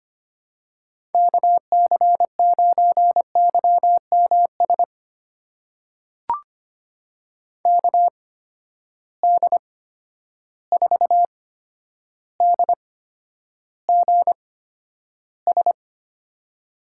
Morse code, heard as a keyed tone, about 25 wpm, spaced as for 11 wpm, 700 Hz. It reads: KC9YMS KD4DGS
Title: USA Call Signs - 8x Character Spacing - 25wpm